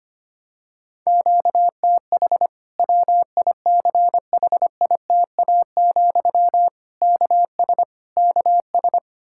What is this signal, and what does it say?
Morse code, heard as a keyed tone, about 25 wpm, 700 Hz. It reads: QTH WICHITA, KS KS
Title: Practice QSO - Exchange 1 - Beginning - 25wpm